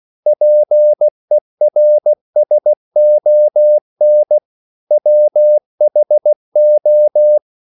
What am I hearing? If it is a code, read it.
Morse code, heard as a keyed tone, about 16 wpm, 600 Hz. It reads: PERSON WHO